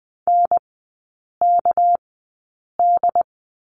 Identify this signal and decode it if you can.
Morse code, heard as a keyed tone, about 20 wpm, spaced as for 8 wpm, 700 Hz. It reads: NKD
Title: 198 Lesson 32e - Letter k - Three Characters 20wpm